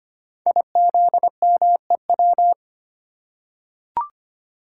Morse code, heard as a keyed tone, about 25 wpm, 700 Hz. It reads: IZMEW
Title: Code Groups - 5 Letters - 25wpm